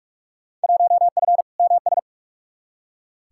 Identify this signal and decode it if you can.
Morse code, heard as a keyed tone, about 46 wpm, 700 Hz. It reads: 1F MS
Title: Field Day Exchanges Encore - 45wpm